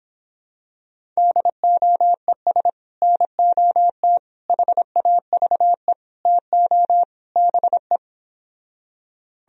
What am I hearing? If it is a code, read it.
Morse code, heard as a keyed tone, about 26 wpm, 700 Hz. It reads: DOES NOT HAVE TO BE